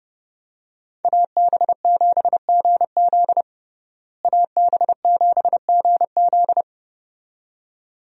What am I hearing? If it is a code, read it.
Morse code, heard as a keyed tone, about 32 wpm, 700 Hz. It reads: AB7GZ AB7GZ
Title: USA Call Signs - 5 Characters - Repeated 2x - 30wpm